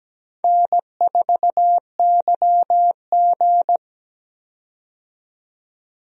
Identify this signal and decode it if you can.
Morse code, heard as a keyed tone, about 17 wpm, 700 Hz. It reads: N4YG
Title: USA Call Signs - 4 Characters - 17wpm